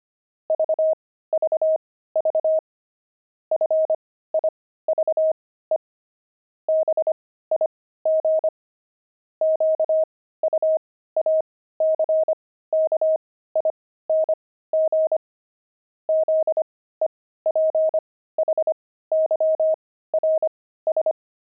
Morse code, heard as a keyed tone, about 25 wpm, spaced as for 15 wpm, 630 Hz. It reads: VVV FIVE BIG QUACKING ZEPHYRS
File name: pangram-fs-15-cs-25-five